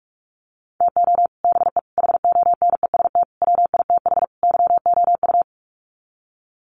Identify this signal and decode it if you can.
Morse code, heard as a keyed tone, about 44 wpm, 700 Hz. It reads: TO BE HONEST WITH YOU
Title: Sets of 5 Words Encore 45wpm